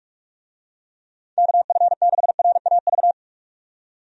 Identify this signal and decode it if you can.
Morse code, heard as a keyed tone, about 46 wpm, 700 Hz. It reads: KF6RAV
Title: USA Call Signs - 6 Characters - 45wpm